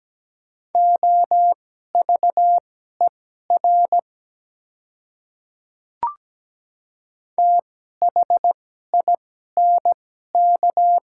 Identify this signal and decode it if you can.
Morse code, heard as a keyed tone, about 17 wpm, spaced as for 12 wpm, 700 Hz. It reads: OVER THINK
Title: 2x Character Spacing - Top 300 Words - 17wpm